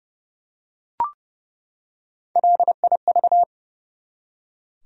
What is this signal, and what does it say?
Morse code, heard as a keyed tone, about 30 wpm, 700 Hz. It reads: LIV
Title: USA Common Names - Max 3 Letters - 30wpm